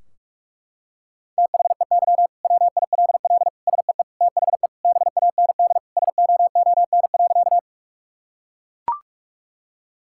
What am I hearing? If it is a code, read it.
Morse code, heard as a keyed tone, about 41 wpm, 700 Hz. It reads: THEY WILL SEE THE BAND SOON.
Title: Sentences from Top 1600 Words 45wpm